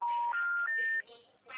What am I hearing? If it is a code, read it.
Special information tone, with noise.
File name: out- -2017